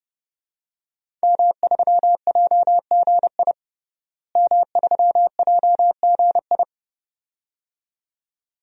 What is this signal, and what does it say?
Morse code, heard as a keyed tone, about 32 wpm, 700 Hz. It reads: M3JGI M3JGI
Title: DXCC-100 Call Signs - 5 Characters - Repeated 2x - 30wpm